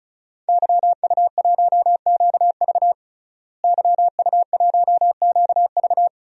Morse code, heard as a keyed tone, about 36 wpm, 700 Hz. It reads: YU1QV YU1QV
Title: DXCC-100 Call Signs - 5 Characters - Repeated 2x - 35wpm